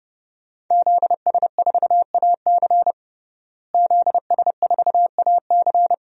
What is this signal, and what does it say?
Morse code, heard as a keyed tone, about 30 wpm, 700 Hz. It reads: ZS4AC ZS4AC
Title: DXCC-100 Call Signs - 5 Characters - Repeated 2x - 30wpm